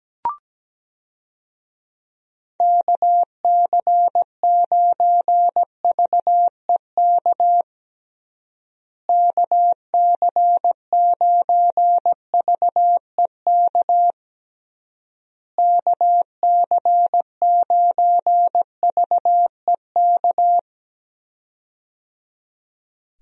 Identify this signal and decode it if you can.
Morse code, heard as a keyed tone, about 17 wpm, 700 Hz. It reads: KC9VEK KC9VEK KC9VEK
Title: USA Call Signs - 6 Characters - Repeated 3x - 17wpm